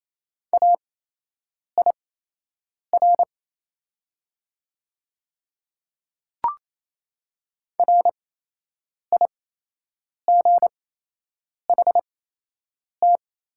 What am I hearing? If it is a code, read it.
Morse code, heard as a keyed tone, about 28 wpm, spaced as for 8 wpm, 700 Hz. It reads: AIR RIGHT
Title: 8x Character Spacing - Top 200 Words - 28wpm